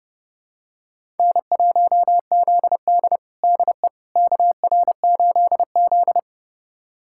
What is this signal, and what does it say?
Morse code, heard as a keyed tone, about 30 wpm, 700 Hz. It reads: N1ZD DE KR8Z
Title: USA Callsign DE Callsign - 4 Characters - 30wpm